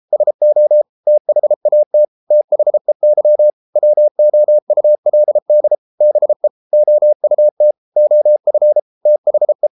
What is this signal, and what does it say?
Morse code, heard as a keyed tone, about 33 wpm, 600 Hz. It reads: SO THAT THEY WOULD BE OUT OF THE